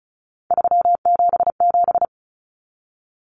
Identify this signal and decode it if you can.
Morse code, heard as a keyed tone, about 34 wpm, 700 Hz. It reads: 377
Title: Common Cut Numbers - 3-Digit 35wpm